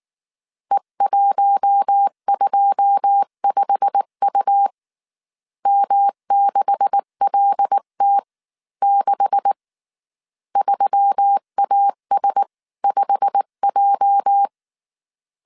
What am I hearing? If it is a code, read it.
Morse code, heard as a keyed tone, about 19 wpm, spaced as for 16 wpm, 800 Hz. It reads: E125U M6LT6 3AS5J
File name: CW / DET / 1 9 1500C S B 18wpm 2551